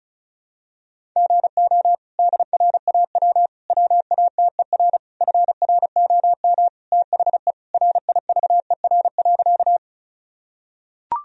Morse code, heard as a keyed tone, about 36 wpm, 700 Hz. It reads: GO DRAW WATER FROM THE RIVER.
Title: Sentences from Top 1000 Words - Review Part 1 - 35wpm